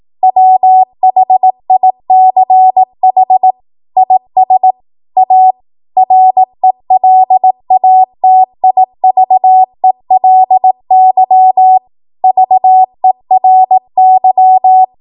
Morse code, heard as a keyed tone, about 18 wpm, 750 Hz. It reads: WHICH IS A RELATIVELY VERY
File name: Morse / 18WPM